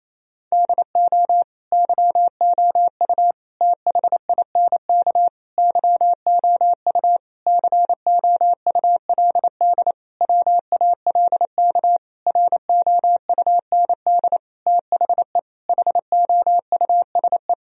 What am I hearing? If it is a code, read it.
Morse code, heard as a keyed tone, about 28 wpm, 700 Hz. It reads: DO YOU THINK YOU COULD WALK ROUND THE HOUSE
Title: Sentences from Top 1000 Words - Review Part 1 - 28wpm